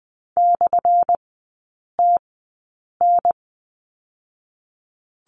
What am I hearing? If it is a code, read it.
Morse code, heard as a keyed tone, about 20 wpm, spaced as for 8 wpm, 700 Hz. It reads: /TN
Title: 178 Lesson 29e - Slash - Three Characters 20wpm